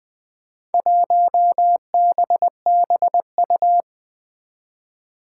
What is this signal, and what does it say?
Morse code, heard as a keyed tone, about 20 wpm, 700 Hz. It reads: 1BBU
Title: Four Challenging Letter-Number Combinations 20wpm